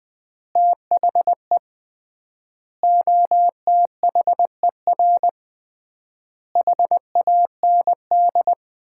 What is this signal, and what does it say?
Morse code, heard as a keyed tone, about 20 wpm, 700 Hz. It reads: THE OTHER HAND
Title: 3x Word-Spacing - Sets of 5 Words - 20wpm